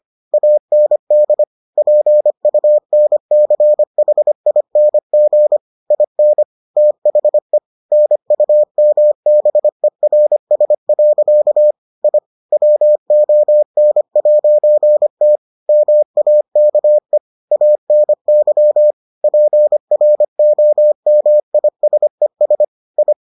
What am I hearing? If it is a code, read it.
Morse code, heard as a keyed tone, about 25 wpm, 600 Hz. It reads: AND PUNCHING IN THE NUMBERS. I WON'T MAKE ANY PROMISES I